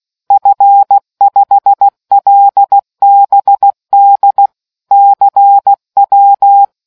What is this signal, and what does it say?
Morse code, heard as a keyed tone, about 16 wpm, 800 Hz. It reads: F5LBD CW